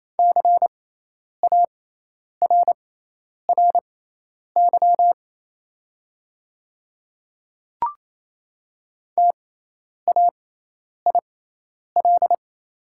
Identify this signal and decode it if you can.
Morse code, heard as a keyed tone, about 27 wpm, spaced as for 10 wpm, 690 Hz. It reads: CARRY TAIL
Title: 6x Character Spacing - Top 500 Words - 28wpm